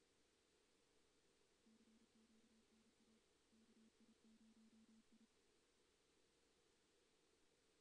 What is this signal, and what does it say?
Morse code, heard as a keyed tone, about 31 wpm, 220 Hz. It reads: CQ NA KI0I
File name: 2016 JAN CW NAQP / 2016 Jan naqp k5ot - 0-3